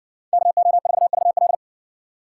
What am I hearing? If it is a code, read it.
Morse code, heard as a keyed tone, about 57 wpm, 700 Hz. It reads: KO4VL